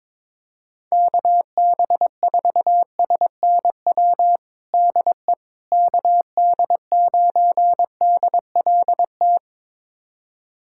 Morse code, heard as a keyed tone, about 21 wpm, 700 Hz. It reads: KB4SNW DE KD9DLT
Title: USA Callsign DE Callsign 22wpm